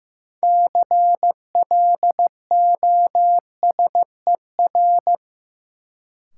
Morse code, heard as a keyed tone, about 15 wpm, 700 Hz. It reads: CLOSER
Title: Sets of 5 Words Encore - New Word Drill - 15wpm